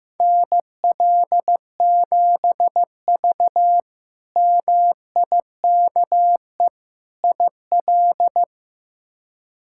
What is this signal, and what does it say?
Morse code, heard as a keyed tone, about 15 wpm, 700 Hz. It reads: NL7V MIKE IL
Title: K1USN SST QSO Practice - 15wpm